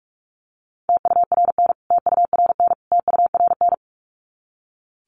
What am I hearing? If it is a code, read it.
Morse code, heard as a keyed tone, about 44 wpm, 700 Hz. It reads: TURN TURN TURN
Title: Top 300 Words Repeated 45wpm